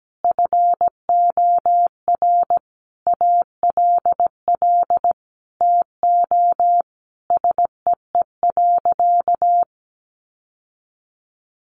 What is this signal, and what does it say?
Morse code, heard as a keyed tone, about 17 wpm, 700 Hz. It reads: FOR ALL TO SEE.
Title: Sentences from Top 1200 Words 17wpm